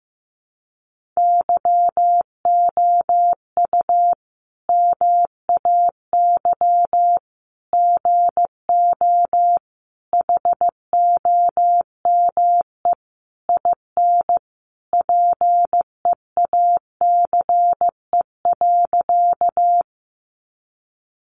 Morse code, heard as a keyed tone, about 15 wpm, 700 Hz. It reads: YOU MAY GO HOME IN PEACE.
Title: Sentences from Top 1500 Words 15wpm